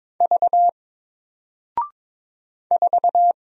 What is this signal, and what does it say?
Morse code, heard as a keyed tone, about 22 wpm, 700 Hz. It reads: V 4
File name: Single-Letter-Number-22wpm